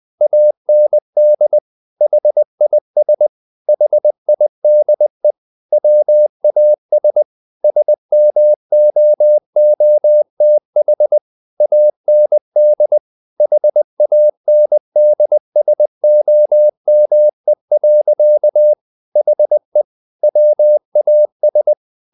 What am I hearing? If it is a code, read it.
Morse code, heard as a keyed tone, about 20 wpm, 600 Hz. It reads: AND HIS HIDE WAS SMOOTH AND HANDSOME. HE WAS
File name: the old man and the sea part 2 20wpm